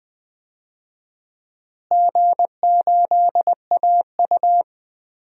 Morse code, heard as a keyed tone, about 20 wpm, 700 Hz. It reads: G8AU
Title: DXCC-100 Call Signs - 4 Characters - Repeated 2x - 20wpm